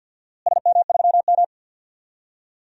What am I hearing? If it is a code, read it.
Morse code, heard as a keyed tone, about 53 wpm, 700 Hz. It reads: SM3M
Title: DXCC-100 Call Signs - 4 Characters - Repeated 3x - 50wpm